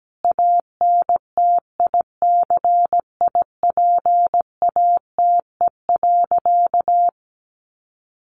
Morse code, heard as a keyed tone, about 17 wpm, 700 Hz. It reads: ANTICIPATE.
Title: Sentences from Top 1500 Words 17wpm